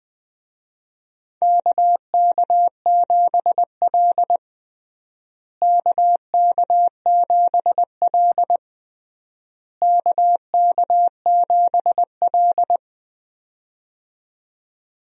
Morse code, heard as a keyed tone, about 20 wpm, 700 Hz. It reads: KK7L KK7L KK7L
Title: USA Call Signs - 4 Characters - Repeated 3x - 20wpm